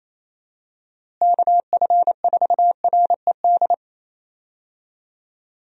Morse code, heard as a keyed tone, about 28 wpm, 700 Hz. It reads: KF4RED
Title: USA Call Signs - 6 Characters - Repeated 3x - 28wpm